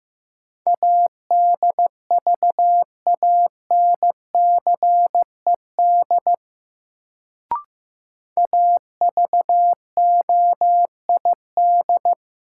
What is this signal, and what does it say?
Morse code, heard as a keyed tone, about 15 wpm, 700 Hz. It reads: ADVANCED AVOID
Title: Sentences from Top 1200 Words - New Word Drill Alt - 15wpm